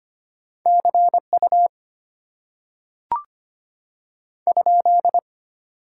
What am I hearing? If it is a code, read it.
Morse code, heard as a keyed tone, about 25 wpm, 700 Hz. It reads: CU ?
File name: Top 90 QSO Elements - 6x Character Spacing - 25wpm